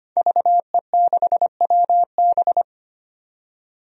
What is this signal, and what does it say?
Morse code, heard as a keyed tone, about 25 wpm, 700 Hz. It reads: VE6WB
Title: DXCC-100 Call Signs - 5 Characters - 25wpm